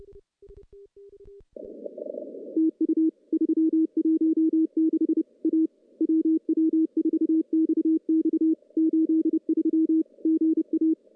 Morse code, heard as a keyed tone, about 30 wpm, 320 Hz. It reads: TU 316 A WW4XX 83 GA
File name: CW-SS-k5tr